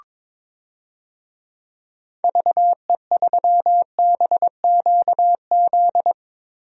Morse code, heard as a keyed tone, about 22 wpm, 700 Hz. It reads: VE3BQZ
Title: DXCC-100 Call Signs - 6 Characters - Repeated 3x - 22wpm